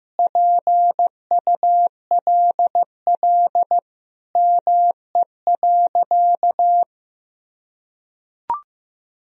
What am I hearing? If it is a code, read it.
Morse code, heard as a keyed tone, about 15 wpm, 700 Hz. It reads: PULL ME.
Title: Sentences from Top 1000 Words - Review Part 2 - 15wpm